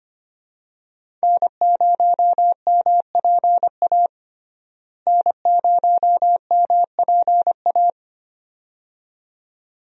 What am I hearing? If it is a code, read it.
Morse code, heard as a keyed tone, about 25 wpm, 700 Hz. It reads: N0MPA N0MPA
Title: USA Call Signs - 5 Characters - Repeated 2x - 25wpm